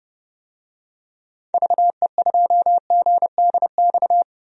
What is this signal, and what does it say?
Morse code, heard as a keyed tone, about 30 wpm, 700 Hz. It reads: VE2GDX
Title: DXCC-100 Call Signs - 6 Characters - Repeated 3x - 30wpm